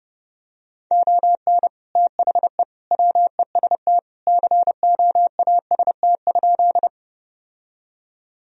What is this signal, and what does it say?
Morse code, heard as a keyed tone, about 30 wpm, 700 Hz. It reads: ON THE WEST COAST?
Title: Sentences from Top 1000 Words - Review Part 4 - 30wpm